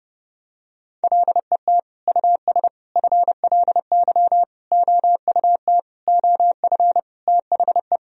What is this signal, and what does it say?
Morse code, heard as a keyed tone, about 30 wpm, 700 Hz. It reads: LET US FLY OUT OF THE
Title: Sentences from Top 1000 Words - Review Part 2 - 30wpm